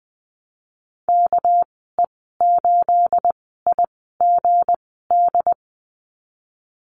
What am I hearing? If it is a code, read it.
Morse code, heard as a keyed tone, about 20 wpm, spaced as for 15 wpm, 700 Hz. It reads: KE8IGD
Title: 219 Lesson 35g - Number 0 - Call Signs 20wpm